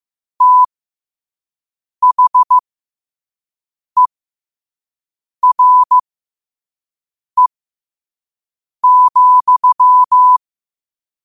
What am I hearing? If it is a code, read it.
Morse code, heard as a keyed tone, about 15 wpm, spaced as for 5 wpm, 1000 Hz. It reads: THERE,
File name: qotdfiles / 5wpm